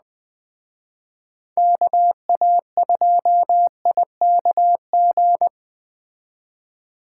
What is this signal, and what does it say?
Morse code, heard as a keyed tone, about 19 wpm, 700 Hz. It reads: KA2IKG